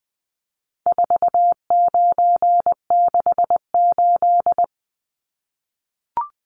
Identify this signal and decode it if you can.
Morse code, heard as a keyed tone, about 20 wpm, 700 Hz. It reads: 4968
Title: Numbers - 4-Digit 20wpm